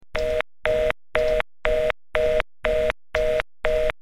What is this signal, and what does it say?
Reorder tone (North America), 480 + 620 Hz, with noise.
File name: busysignal